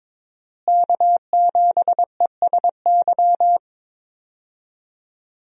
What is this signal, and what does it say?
Morse code, heard as a keyed tone, about 22 wpm, 700 Hz. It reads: K7ESY
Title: USA Call Signs - 5 Characters - Repeated 2x - 22wpm